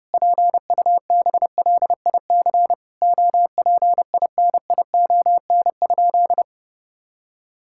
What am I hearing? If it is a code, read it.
Morse code, heard as a keyed tone, about 32 wpm, 700 Hz. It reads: PUBLIC OPINION?
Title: Sentences from Top 1600 Words 30wpm